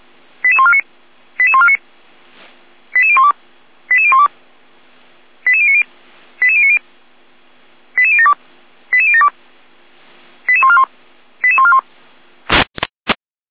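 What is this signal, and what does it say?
Five-tone selective call (ZVEI), heard as quick runs of five tones, tones of about 70 ms. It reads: 88149 88149 88012 88012 88090 88090 88072 88072 88141 88141
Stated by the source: Five-tone ZVEI selective call received by SDR.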